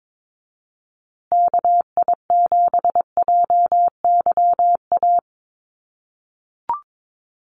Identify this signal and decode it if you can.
Morse code, heard as a keyed tone, about 22 wpm, 700 Hz. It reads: KI7JYA
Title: USA Call Signs Repeated 3x 22wpm